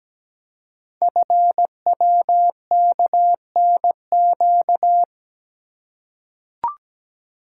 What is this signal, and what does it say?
Morse code, heard as a keyed tone, about 17 wpm, 700 Hz. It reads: FWKNQ
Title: Code Groups - 5 Letters - 17wpm